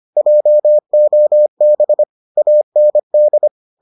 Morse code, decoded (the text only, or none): JOB AND